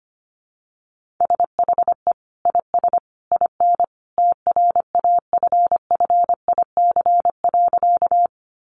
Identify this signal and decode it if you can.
Morse code, heard as a keyed tone, about 25 wpm, 700 Hz. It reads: SHE IS IN TRAFFIC.
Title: Sentences from Top 1300 Words 25wpm